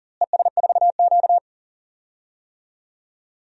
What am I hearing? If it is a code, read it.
Morse code, heard as a keyed tone, about 40 wpm, 700 Hz. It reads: ES4Q
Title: DXCC-100 Call Signs - 4 Characters - Repeated 2x - 40wpm